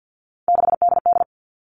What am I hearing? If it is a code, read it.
Morse code, heard as a keyed tone, about 50 wpm, 700 Hz. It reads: 6DD
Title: All Cut Numbers - 3-Digit 50wpm